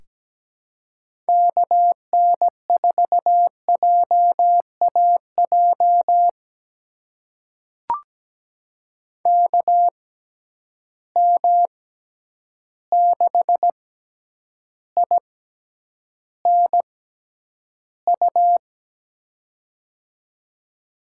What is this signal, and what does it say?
Morse code, heard as a keyed tone, about 17 wpm, spaced as for 9 wpm, 700 Hz. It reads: KN4JAJ KM6INU
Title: USA Call Signs - 6x Character Spacing - 17wpm